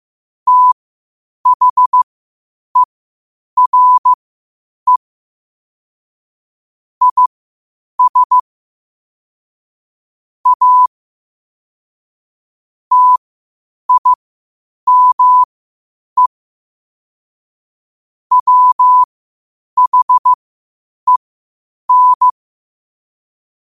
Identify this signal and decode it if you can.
Morse code, heard as a keyed tone, about 15 wpm, spaced as for 8 wpm, 1000 Hz. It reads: THERE IS A TIME WHEN
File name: qotdfiles / 8wpm